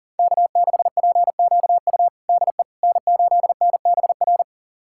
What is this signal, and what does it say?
Morse code, heard as a keyed tone, about 40 wpm, 700 Hz. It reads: K6PQU DE N8NBR
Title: USA Callsign DE Callsign - 5 Characters - 40wpm